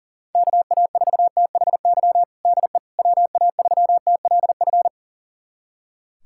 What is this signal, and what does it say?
Morse code, heard as a keyed tone, about 40 wpm, 700 Hz. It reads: KA4THY DE WA3TLF
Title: USA Callsign DE Callsign 40wpm